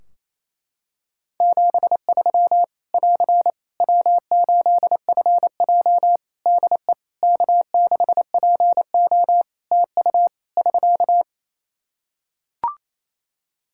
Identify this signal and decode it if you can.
Morse code, heard as a keyed tone, about 28 wpm, 700 Hz. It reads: 73 <AR> W8FJ DE K6PO TU <SK>
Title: Practice QSO - Exchange 4 - Ending - 28wpm